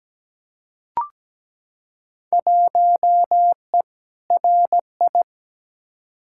Morse code, heard as a keyed tone, about 17 wpm, 700 Hz. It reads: 1E RI
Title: Field Day Exchanges Encore - 17wpm